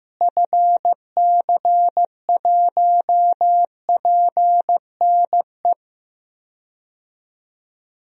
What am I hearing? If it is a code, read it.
Morse code, heard as a keyed tone, about 15 wpm, 700 Hz. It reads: FC1PNE
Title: DXCC-100 Call Signs - 6 Characters - Repeated 2x - 15wpm